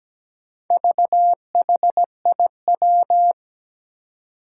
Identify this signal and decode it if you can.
Morse code, heard as a keyed tone, about 17 wpm, 700 Hz. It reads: VHIW
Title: Four Challenging Letter-Number Combinations 17wpm